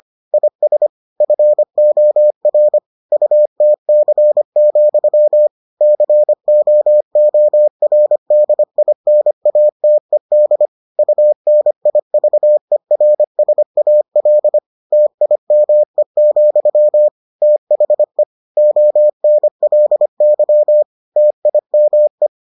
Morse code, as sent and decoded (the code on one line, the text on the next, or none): .. ... / ..-. --- .-. / ..- - -.-. --..-- / -.-. --- --- .-. -.. .. -. .- - . -.. / ..- -. .. ...- . .-. ... .- .-.. / - .. -- . --..-- / - .... / --- -. .-.. -.-- / - .. -- .
IS FOR UTC, COORDINATED UNIVERSAL TIME, THE ONLY TIME